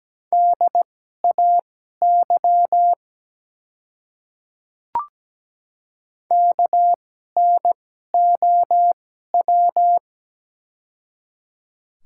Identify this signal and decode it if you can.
Morse code, heard as a keyed tone, about 17 wpm, spaced as for 12 wpm, 700 Hz. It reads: DAY KNOW
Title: 2x Character Spacing - Top 100 Words - 17wpm